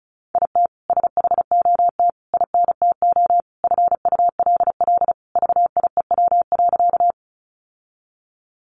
Morse code, heard as a keyed tone, about 34 wpm, 700 Hz. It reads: IT SHOT INTO FULL VIEW.
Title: Sentences from Top 1100 Words 35wpm